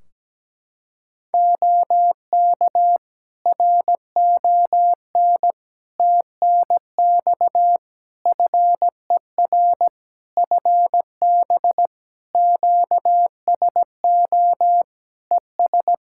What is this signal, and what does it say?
Morse code, heard as a keyed tone, about 17 wpm, 700 Hz. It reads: OK RON TNX FER FB QSO ES
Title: Practice QSO - Exchange 4 - Ending - 17wpm